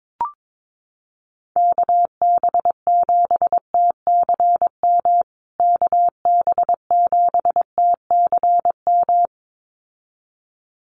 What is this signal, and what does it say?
Morse code, heard as a keyed tone, about 22 wpm, 700 Hz. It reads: KB7TCM KB7TCM
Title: USA Call Signs Repeated 2x 22wpm